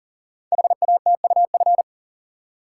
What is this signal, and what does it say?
Morse code, heard as a keyed tone, about 40 wpm, 690 Hz. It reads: HATUF
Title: Code Groups - 5 Letters - 40wpm